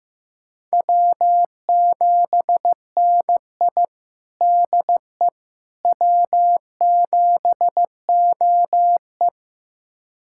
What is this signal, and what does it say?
Morse code, heard as a keyed tone, about 15 wpm, 700 Hz. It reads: W7NI DE W7OE
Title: USA Callsign DE Callsign - 4 Characters - 15wpm